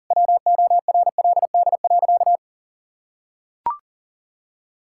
Morse code, heard as a keyed tone, about 37 wpm, 700 Hz. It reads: WORLD.
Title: Sentences from Top 1000 Words - Review Part 5 - 40wpm